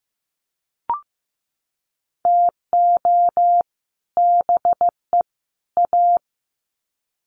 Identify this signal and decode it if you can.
Morse code, heard as a keyed tone, about 15 wpm, 700 Hz. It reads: TO BE A
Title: Sets of 3 Words 15wpm